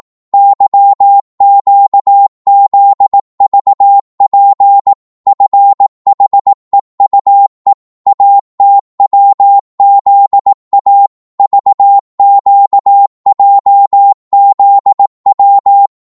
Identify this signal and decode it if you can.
Morse code, heard as a keyed tone, about 18 wpm, 800 Hz. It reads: YQZVP FHEUE ATWZA VQJZW